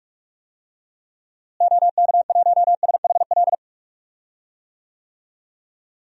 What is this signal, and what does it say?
Morse code, heard as a keyed tone, about 46 wpm, 700 Hz. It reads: OK1SHL
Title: DXCC-100 Call Signs - 6 Characters - Repeated 3x - 45wpm